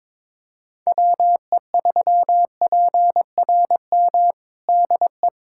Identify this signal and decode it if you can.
Morse code, heard as a keyed tone, about 22 wpm, 700 Hz. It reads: WE3PRM DE
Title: USA Callsign DE Callsign - 6 Characters - 22wpm